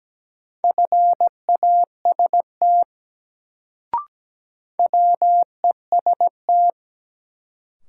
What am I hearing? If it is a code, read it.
Morse code, heard as a keyed tone, about 17 wpm, 700 Hz. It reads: FAST WEST
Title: Sentences from Top 400 Words - New Word Drill - 17wpm